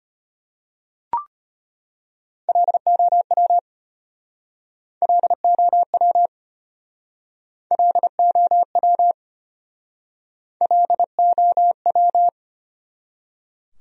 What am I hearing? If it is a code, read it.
Morse code, heard as a keyed tone, about 32 wpm, 700 Hz. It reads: LOW LOW LOW LOW